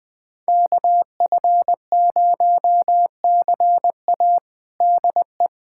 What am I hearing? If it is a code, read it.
Morse code, heard as a keyed tone, about 21 wpm, 700 Hz. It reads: KF0CA DE
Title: USA Callsign DE Callsign - 5 Characters - 20wpm